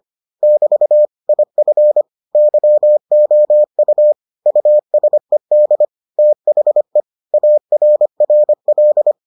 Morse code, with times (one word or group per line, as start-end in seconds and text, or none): 0.4-1.1 <BT>
1.3-2.0 IF
2.4-4.1 YOU
4.5-5.8 USED
6.2-7.0 THE
7.3-9.1 ARRL